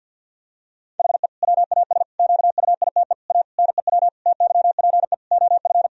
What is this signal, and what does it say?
Morse code, heard as a keyed tone, about 50 wpm, 700 Hz. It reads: HE WAS QUITE A NEW TYPE OF
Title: Sentences from Top 1000 Words - Review Part 3 - 50wpm